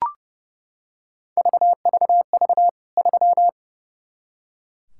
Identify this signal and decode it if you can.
Morse code, heard as a keyed tone, about 30 wpm, 700 Hz. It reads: VVV 3
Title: K1USN SST QSO Practice 30wpm